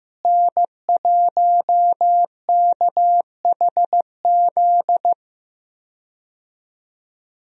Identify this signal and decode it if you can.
Morse code, heard as a keyed tone, about 15 wpm, 700 Hz. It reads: N1KHZ
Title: USA Call Signs - 5 Characters - Repeated 2x - 15wpm